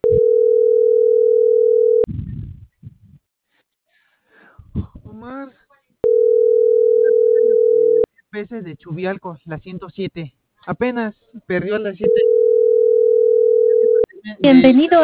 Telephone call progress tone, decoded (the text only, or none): ringback tone